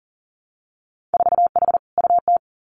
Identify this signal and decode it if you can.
Morse code, heard as a keyed tone, about 38 wpm, 700 Hz. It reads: 4H UT
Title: Winter Field Day Exchanges Encore 40wpm